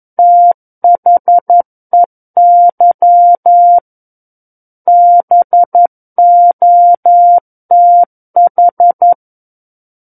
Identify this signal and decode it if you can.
Morse code, heard as a keyed tone, about 11 wpm, 700 Hz. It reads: THEY BOTH
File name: 11 WPM